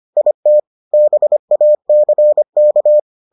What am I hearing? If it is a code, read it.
Morse code, heard as a keyed tone, about 25 wpm, 600 Hz. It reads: IT BACK